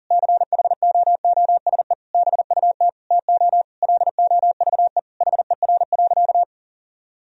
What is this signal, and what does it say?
Morse code, heard as a keyed tone, about 40 wpm, 700 Hz. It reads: CHOOSE BUT TO LOVE HER.
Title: Sentences from Top 1000 Words - Review Part 4 - 40wpm